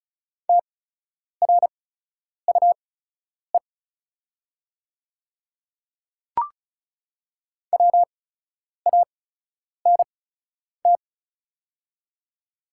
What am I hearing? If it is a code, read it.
Morse code, heard as a keyed tone, about 36 wpm, spaced as for 10 wpm, 700 Hz. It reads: TRUE WANT
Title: 8x Character Spacing - Top 500 Words - 35wpm